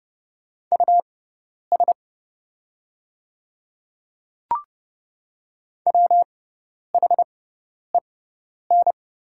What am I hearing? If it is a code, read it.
Morse code, heard as a keyed tone, about 30 wpm, spaced as for 10 wpm, 700 Hz. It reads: US WHEN
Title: 6x Character Spacing - Top 200 Words - 30wpm